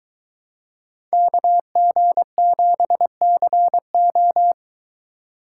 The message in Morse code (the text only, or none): KG7CO